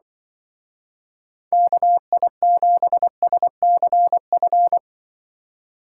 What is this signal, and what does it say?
Morse code, heard as a keyed tone, about 25 wpm, 700 Hz. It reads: KI7SCF
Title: Speed Racing - USA Call Signs 20wpm